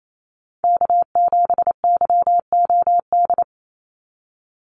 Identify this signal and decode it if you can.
Morse code, heard as a keyed tone, about 28 wpm, 700 Hz. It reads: K7YOD
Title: USA Call Signs 28wpm